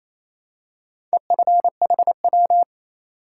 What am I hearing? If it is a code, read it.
Morse code, heard as a keyed tone, about 28 wpm, 700 Hz. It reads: EFHW
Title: Top 180 QSO Elements 28wpm